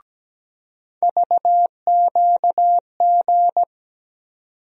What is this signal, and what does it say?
Morse code, heard as a keyed tone, about 17 wpm, 700 Hz. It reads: VQG
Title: USA Call Sign Postfixes - 3 Letters 17wpm